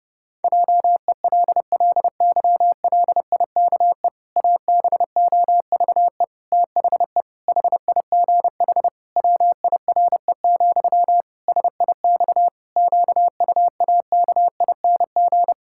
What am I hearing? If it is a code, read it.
Morse code, heard as a keyed tone, about 30 wpm, 700 Hz. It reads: JELLYLIKE ABOVE THE HIGH WIRE, SIX QUAKING
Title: Pangram Challenge v2 - 30wpm